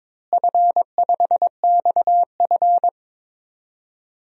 Morse code, heard as a keyed tone, about 22 wpm, 700 Hz. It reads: F5XF
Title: DXCC-100 Call Signs - 4 Characters - Repeated 3x - 22wpm